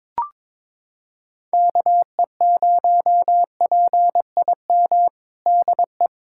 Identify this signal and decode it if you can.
Morse code, heard as a keyed tone, about 21 wpm, 700 Hz. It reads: KE0PIM DE
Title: USA Callsign DE Callsign - 6 Characters - 22wpm